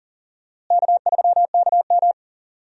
Morse code, heard as a keyed tone, about 40 wpm, 700 Hz. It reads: K3KM
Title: USA Call Signs - 4 Characters - Repeated 3x - 40wpm